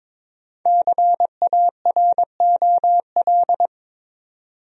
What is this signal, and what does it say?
Morse code, heard as a keyed tone, about 22 wpm, 700 Hz. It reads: CAROL
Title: USA Common Names - Max 5 Letters - 22wpm